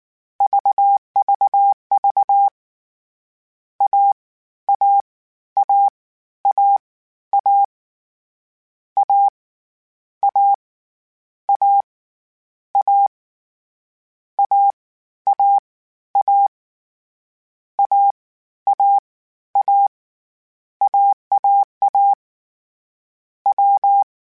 Morse code, heard as a keyed tone, about 19 wpm, 800 Hz. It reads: VVV A A A A A A A A A A A A A A A AAA W
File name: Difficult Combos / Difficult Combos-mp3s / AWJ @18